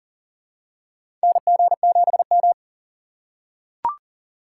Morse code, heard as a keyed tone, about 40 wpm, 700 Hz. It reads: NG7M
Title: USA Call Signs - 4 Characters - 3x Character Spacing - 40wpm